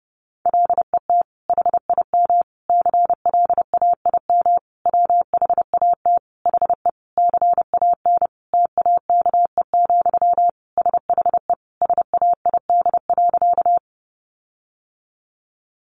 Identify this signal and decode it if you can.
Morse code, heard as a keyed tone, about 28 wpm, 700 Hz. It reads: LET HIM CLAIM WHAT HE CAN TAKE, SHE SAID.
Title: Sentences from Top 1500 Words 30wpm